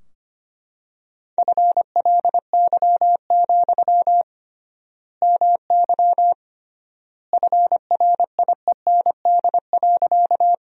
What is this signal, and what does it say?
Morse code, heard as a keyed tone, about 25 wpm, 700 Hz. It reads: FLY, MY FRIEND.
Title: 3x Word-Spacing - Sentences from Top 500 Words - 25wpm